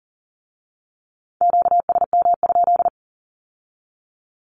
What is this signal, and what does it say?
Morse code, heard as a keyed tone, about 38 wpm, 700 Hz. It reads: QSM?
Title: Q-Codes with Spoken Meaning - 40wpm